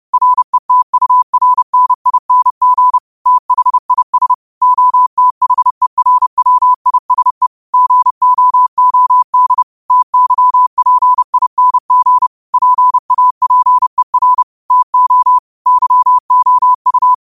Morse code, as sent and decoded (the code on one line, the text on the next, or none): .-. - ..- .-. -. .. -. --. / - .... .. ... / --- - .... .-. .-- .. ... / --. --- --- -.. / - -.-- .--. .. -. --. / .--. .- .--. .-. / - --- / -.-- --- ..-
RETURNING THIS OTHERWISE GOOD TYPING PAPER TO YOU